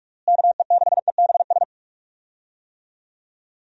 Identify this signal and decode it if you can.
Morse code, heard as a keyed tone, about 46 wpm, 700 Hz. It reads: KE6EBS
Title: USA Call Signs - 6 Characters - Repeated 3x - 45wpm